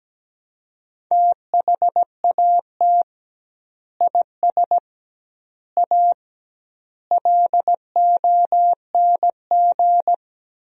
Morse code, heard as a keyed tone, about 17 wpm, 700 Hz. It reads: THAT IS A LONG
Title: 2x Word-Spacing - Sentences from Top 100 Words - 17wpm